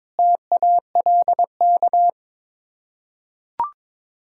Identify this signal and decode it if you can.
Morse code, heard as a keyed tone, about 21 wpm, 700 Hz. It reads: TALK
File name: Build A Phrase - 5-Word Phrases - 22wpm